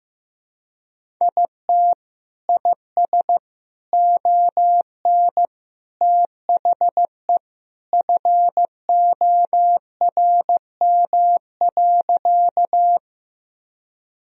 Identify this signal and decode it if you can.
Morse code, heard as a keyed tone, about 15 wpm, 700 Hz. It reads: IT IS ON THE FORM.
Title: Sentences from Top 1000 Words - Review Part 1 - 15wpm